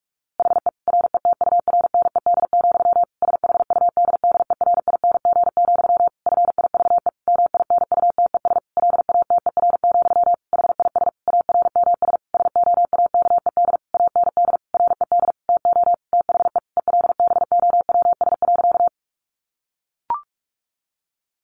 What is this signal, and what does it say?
Morse code, heard as a keyed tone, about 44 wpm, 700 Hz. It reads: HE RETURNED, SHUDDERING, FIVE MINUTES LATER, HIS ARMS SOAKED AND RED TO THE ELBOWS.
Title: A Sound Of Thunder 45wpm